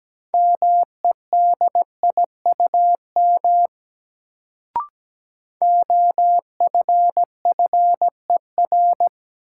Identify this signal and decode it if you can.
Morse code, heard as a keyed tone, about 17 wpm, 700 Hz. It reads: MEDIUM OFFER
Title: Sentences from Top 900 Words - New Word Drill - 17wpm